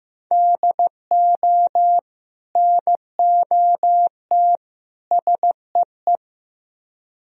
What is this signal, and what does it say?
Morse code, heard as a keyed tone, about 18 wpm, 700 Hz. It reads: DO NOT SEE
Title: Speed-Racing - Sets of 3 Words 15wpm